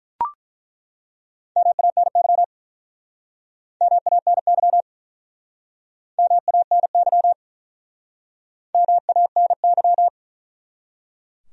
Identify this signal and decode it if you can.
Morse code, heard as a keyed tone, about 48 wpm, 700 Hz. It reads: MANY MANY MANY MANY